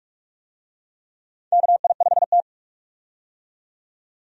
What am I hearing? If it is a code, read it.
Morse code, heard as a keyed tone, about 43 wpm, 700 Hz. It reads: KI5T
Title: USA Call Signs - 4 Characters - 45wpm